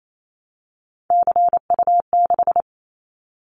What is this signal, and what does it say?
Morse code, heard as a keyed tone, about 28 wpm, 700 Hz. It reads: CU6
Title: Callsign Prefixes for Top 50 DXCC - 28wpm